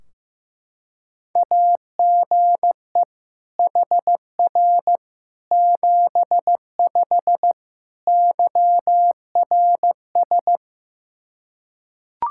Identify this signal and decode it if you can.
Morse code, heard as a keyed tone, about 15 wpm, 700 Hz. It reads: AGE HR 75 YRS
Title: K1USN SST QSO Practice 15wpm